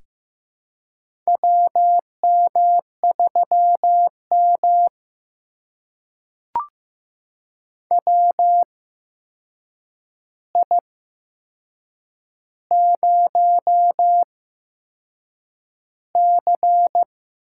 Morse code, heard as a keyed tone, about 15 wpm, spaced as for 6 wpm, 700 Hz. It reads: WM3M WI0C
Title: USA Call Signs - 4 Characters - 8x Character Spacing - 15wpm